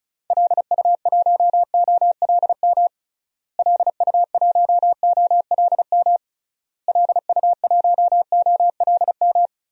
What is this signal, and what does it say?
Morse code, heard as a keyed tone, about 36 wpm, 700 Hz. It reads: LU1OLM LU1OLM LU1OLM
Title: DXCC-100 Call Signs - 6 Characters - Repeated 3x - 35wpm